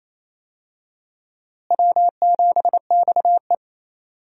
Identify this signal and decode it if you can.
Morse code, heard as a keyed tone, about 27 wpm, 700 Hz. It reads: W7XE
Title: USA Call Signs - 4 Characters - Repeated 3x - 28wpm